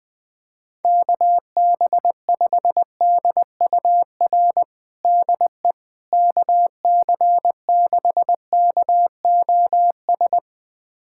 Morse code, heard as a keyed tone, about 21 wpm, 700 Hz. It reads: KB5DUR DE KC6KOS
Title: USA Callsign DE Callsign - 6 Characters - 20wpm